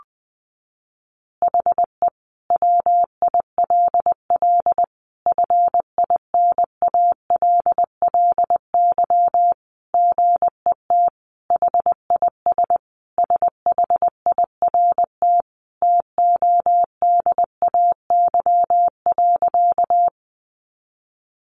Morse code, heard as a keyed tone, about 20 wpm, 700 Hz. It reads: HE WILL FINALLY GET HIS SHIRT TODAY.
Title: Additional Easy Phrases 20wpm